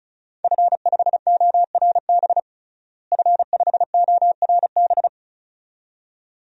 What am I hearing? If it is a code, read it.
Morse code, heard as a keyed tone, about 36 wpm, 700 Hz. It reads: F5ORB F5ORB
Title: DXCC-100 Call Signs - 5 Characters - Repeated 2x - 35wpm